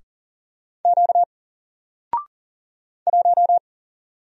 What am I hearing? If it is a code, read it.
Morse code, heard as a keyed tone, about 40 wpm, 700 Hz. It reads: Q 1
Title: Single Letter-Number Alt 40wpm